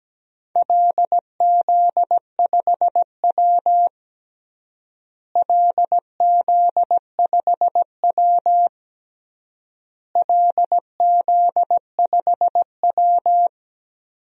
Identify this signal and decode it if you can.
Morse code, heard as a keyed tone, about 17 wpm, 700 Hz. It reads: LZ5W LZ5W LZ5W
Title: DXCC-100 Call Signs - 4 Characters - Repeated 3x - 17wpm